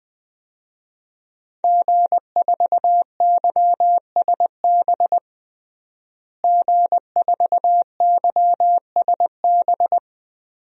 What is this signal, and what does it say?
Morse code, heard as a keyed tone, about 20 wpm, 700 Hz. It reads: G4YSB G4YSB
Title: DXCC-100 Call Signs - 5 Characters - Repeated 2x - 20wpm